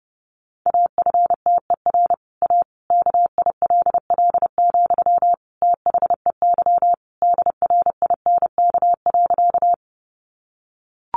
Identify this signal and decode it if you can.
Morse code, heard as a keyed tone, about 30 wpm, 700 Hz. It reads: AFTER A KILL, THEY DRINK.
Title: Sentences from Top 1500 Words 30wpm